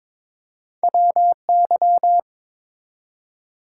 Morse code, heard as a keyed tone, about 23 wpm, 700 Hz. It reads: WY
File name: practice / 22wpm / ARRL Sections v2 - 22wpm